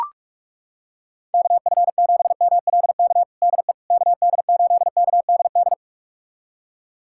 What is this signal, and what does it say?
Morse code, heard as a keyed tone, about 42 wpm, 700 Hz. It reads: KF7MLK DE KD8KDD
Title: USA Callsign DE Callsign 45wpm